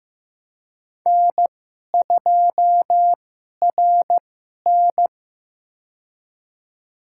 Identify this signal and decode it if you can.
Morse code, heard as a keyed tone, about 15 wpm, spaced as for 11 wpm, 700 Hz. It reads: N2RN
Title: USA Call Signs - 4 Characters - 2x Character Spacing - 15wpm